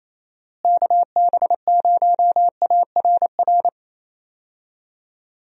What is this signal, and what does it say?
Morse code, heard as a keyed tone, about 28 wpm, 700 Hz. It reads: KB0ARR
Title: USA Call Signs - 6 Characters - Repeated 3x - 28wpm